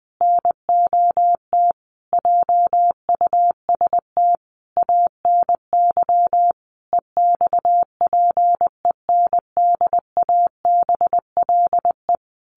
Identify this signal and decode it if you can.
Morse code, heard as a keyed tone, about 20 wpm, 700 Hz. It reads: NOT JUST ANY EXPENDABLE
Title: A Sound Of Thunder 20wpm